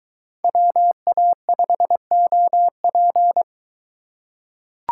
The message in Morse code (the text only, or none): WA5OP